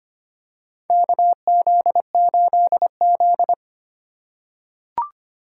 Morse code, heard as a keyed tone, about 25 wpm, 690 Hz. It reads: KZ8Z
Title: USA Call Signs - 4 Characters - 4x Character Spacing - 25wpm